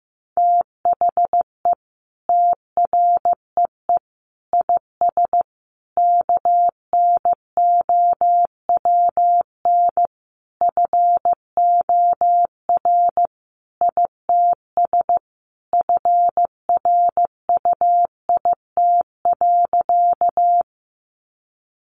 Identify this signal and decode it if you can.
Morse code, heard as a keyed tone, about 15 wpm, 700 Hz. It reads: THE TREE IS KNOWN FOR ITS FRUIT.
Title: Sentences from Top 1300 Words 15wpm